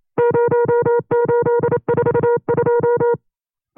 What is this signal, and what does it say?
Morse code, heard as a keyed tone, about 28 wpm, 480 Hz. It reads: 0842
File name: FreeFiles / CW-practice / 25WPM-side1